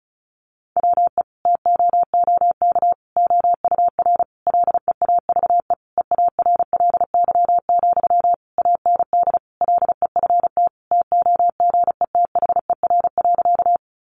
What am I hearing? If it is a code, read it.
Morse code, heard as a keyed tone, about 34 wpm, 700 Hz. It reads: WE TOOK OUR LEAVE EARLY, AND LEFT TOGETHER.
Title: Sentences from Top 400 Words 35wpm